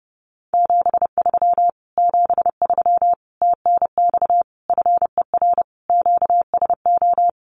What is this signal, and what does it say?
Morse code, heard as a keyed tone, about 30 wpm, 700 Hz. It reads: 73 73 TNX FER QSO
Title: Common QSO Phrases Encore 30wpm